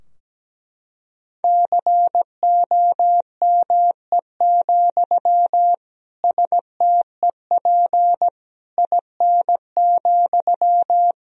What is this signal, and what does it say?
Morse code, heard as a keyed tone, about 17 wpm, 700 Hz. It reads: COME, STEP IN,
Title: Sentences from Top 1000 Words - Review Part 2 - 17wpm